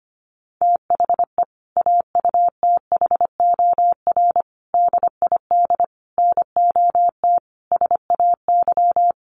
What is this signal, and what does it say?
Morse code, heard as a keyed tone, about 25 wpm, 700 Hz. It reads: THE AUTHOR DID NOT SAY
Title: Sentences from Top 1200 Words 25wpm